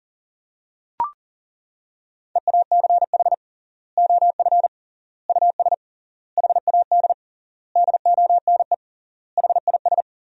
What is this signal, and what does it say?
Morse code, heard as a keyed tone, about 40 wpm, 700 Hz. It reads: EACH OF US HAD DONE HIS
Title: 3x Word-Spacing - Sentences from Top 500 Words - 40wpm